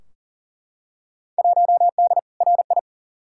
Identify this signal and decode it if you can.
Morse code, heard as a keyed tone, about 37 wpm, 700 Hz. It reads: 1D RI
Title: Field Day Exchanges - 40wpm